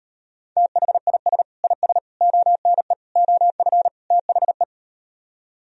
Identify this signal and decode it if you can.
Morse code, heard as a keyed tone, about 38 wpm, 700 Hz. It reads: THIS IS ONE OF THE